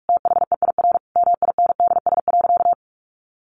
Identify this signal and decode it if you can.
Morse code, heard as a keyed tone, about 44 wpm, 700 Hz. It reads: THEIR MINDS.
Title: Easy Phrases 45wpm